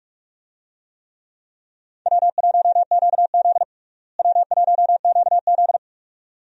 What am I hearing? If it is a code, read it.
Morse code, heard as a keyed tone, about 44 wpm, 700 Hz. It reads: W1QZ W1QZ
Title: USA Call Signs - 4 Characters - Repeated 2x - 45wpm